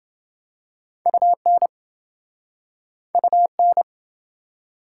Morse code, heard as a keyed tone, about 28 wpm, 700 Hz. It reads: UN UN